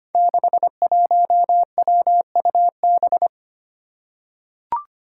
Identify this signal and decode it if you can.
Morse code, heard as a keyed tone, about 25 wpm, 700 Hz. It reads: 61WUB
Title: Five Challenging Letter-Number Combinations 25wpm